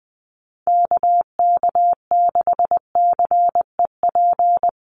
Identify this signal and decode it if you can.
Morse code, heard as a keyed tone, about 20 wpm, 700 Hz. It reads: KK6CEP
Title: USA Call Signs Repeated 3x 20wpm